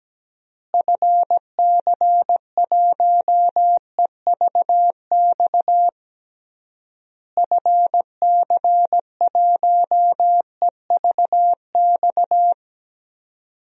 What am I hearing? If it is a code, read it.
Morse code, heard as a keyed tone, about 17 wpm, 700 Hz. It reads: FC1EVX FC1EVX
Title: DXCC-100 Call Signs - 6 Characters - Repeated 3x - 17wpm